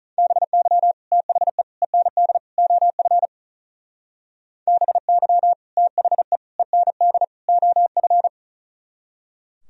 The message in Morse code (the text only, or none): BY THE END OF BY THE END OF